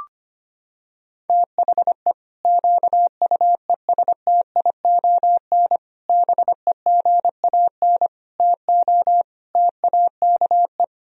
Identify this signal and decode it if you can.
Morse code, heard as a keyed tone, about 25 wpm, 700 Hz. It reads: THE QUESTION BEGAN TO TAKE
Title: Sentences from Top 1000 Words - Review Part 3 - 25wpm